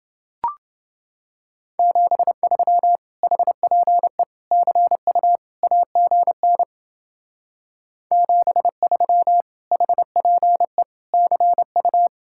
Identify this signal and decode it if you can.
Morse code, heard as a keyed tone, about 30 wpm, 700 Hz. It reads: 73 HPE CU AGN 73 HPE CU